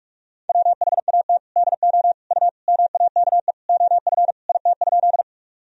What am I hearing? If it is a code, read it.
Morse code, heard as a keyed tone, about 41 wpm, 700 Hz. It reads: WHAT DO U MAKE OF IT?
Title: Sentences from Top 100 Words and Abbreviations 45wpm